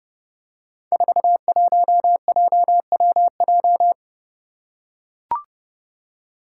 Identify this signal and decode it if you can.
Morse code, heard as a keyed tone, about 30 wpm, 700 Hz. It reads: 41JWJ
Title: Five Challenging Letter-Number Combinations 30wpm